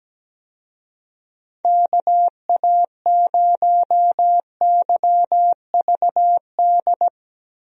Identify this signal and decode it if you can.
Morse code, heard as a keyed tone, about 17 wpm, 700 Hz. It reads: KA0YVD
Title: USA Call Signs - 6 Characters - 17wpm